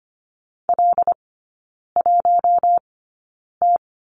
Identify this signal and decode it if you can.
Morse code, heard as a keyed tone, about 25 wpm, spaced as for 9 wpm, 700 Hz. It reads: L1T
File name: course / 25:10wpm / 100 Lesson 17e - Letter u - Three Characters 25wpm